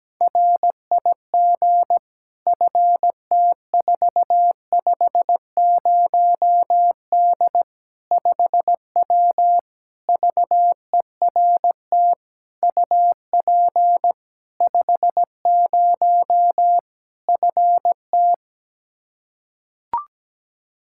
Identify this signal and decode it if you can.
Morse code, heard as a keyed tone, about 17 wpm, 700 Hz. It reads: RIG FT450D 5W VERT UP 50 FT
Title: Practice QSO - Exchange 2 - Info - 17wpm